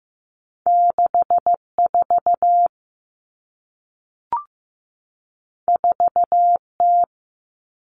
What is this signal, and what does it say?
Morse code, heard as a keyed tone, about 15 wpm, 700 Hz. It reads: 64 4T
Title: All Cut Numbers - 2-Digit 15wpm